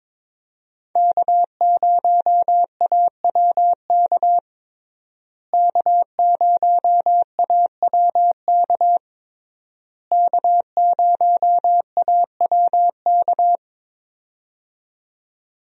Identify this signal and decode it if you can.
Morse code, heard as a keyed tone, about 22 wpm, 700 Hz. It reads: K0AWK K0AWK K0AWK
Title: USA Call Signs - 5 Characters - Repeated 3x - 22wpm